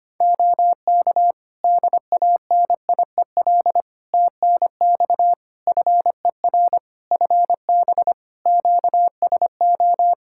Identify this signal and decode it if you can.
Morse code, heard as a keyed tone, about 25 wpm, 700 Hz. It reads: OK DANIEL TNX FER FB QSO
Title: Practice QSO - Exchange 4 - Ending - 25wpm